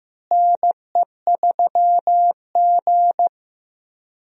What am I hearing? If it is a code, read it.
Morse code, heard as a keyed tone, about 15 wpm, 700 Hz. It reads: NE3G
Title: USA Call Signs - 4 Characters - Repeated 3x - 15wpm